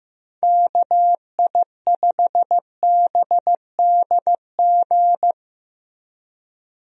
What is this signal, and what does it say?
Morse code, heard as a keyed tone, about 15 wpm, 700 Hz. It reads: KI5BDG
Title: USA Call Signs - 6 Characters - Repeated 2x - 15wpm